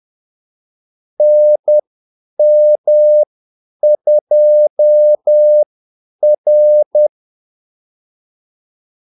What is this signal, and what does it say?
Morse code, heard as a keyed tone, about 10 wpm, spaced as for 8 wpm, 600 Hz. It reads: NM2R
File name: Callsigns-10-8wpm